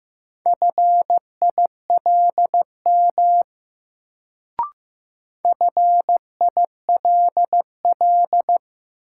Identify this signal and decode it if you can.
Morse code, heard as a keyed tone, about 15 wpm, 700 Hz. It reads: FILM FILL
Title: Sentences from Top 1000 Words - Review Part 3 - New Word Drill - 15wpm